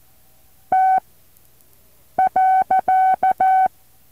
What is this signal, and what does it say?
Morse code, heard as a keyed tone, about 14 wpm, spaced as for 6 wpm, 760 Hz. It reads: T.
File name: intl morse code2